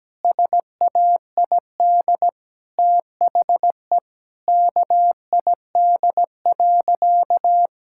Morse code, heard as a keyed tone, about 17 wpm, 700 Hz. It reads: SAID THE KID.
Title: Sentences from Top 1600 Words 17wpm